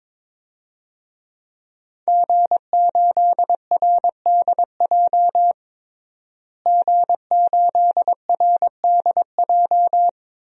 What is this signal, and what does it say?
Morse code, heard as a keyed tone, about 22 wpm, 700 Hz. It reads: G8RDJ G8RDJ
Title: DXCC-100 Call Signs - 5 Characters - Repeated 2x - 22wpm